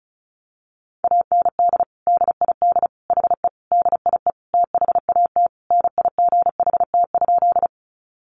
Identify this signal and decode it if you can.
Morse code, heard as a keyed tone, about 34 wpm, 700 Hz. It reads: AND DID HE DIE THAT NIGHT?
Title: Sentences from Top 1300 Words 35wpm